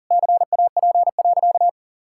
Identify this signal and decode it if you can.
Morse code, heard as a keyed tone, about 40 wpm, 700 Hz. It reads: CAP.
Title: Sentences from Top 1600 Words 40wpm